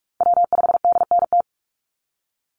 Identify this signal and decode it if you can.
Morse code, heard as a keyed tone, about 46 wpm, 700 Hz. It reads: W5DNT
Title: Call Signs - 45wpm